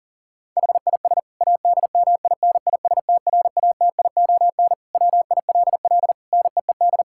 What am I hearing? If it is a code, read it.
Morse code, heard as a keyed tone, about 40 wpm, 700 Hz. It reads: HIS ADMINISTRATION WILL NEED
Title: Sentences from Top 1600 Words 40wpm